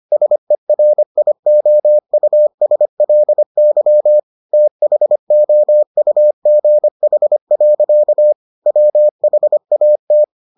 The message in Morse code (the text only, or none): SERIOUSLY THOUGH. WHAT